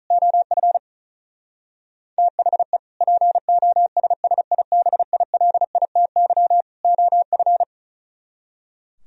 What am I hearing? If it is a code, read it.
Morse code, heard as a keyed tone, about 42 wpm, 700 Hz. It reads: OF THE POSSIBILITY OF
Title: Speed-Racing - Sets of 3 Words 35wpm